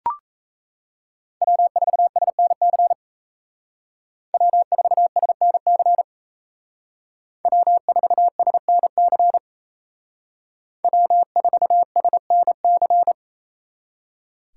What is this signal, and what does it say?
Morse code, heard as a keyed tone, about 36 wpm, 700 Hz. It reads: W4SNC W4SNC W4SNC W4SNC